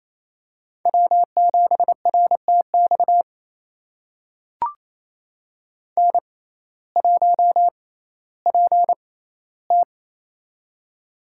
Morse code, heard as a keyed tone, about 28 wpm, spaced as for 15 wpm, 700 Hz. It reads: W7RTX N1PT
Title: USA Call Signs - 6x Character Spacing - 28wpm